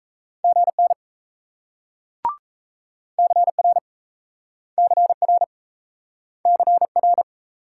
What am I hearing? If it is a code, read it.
Morse code, heard as a keyed tone, about 38 wpm, 700 Hz. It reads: GN CR CR CR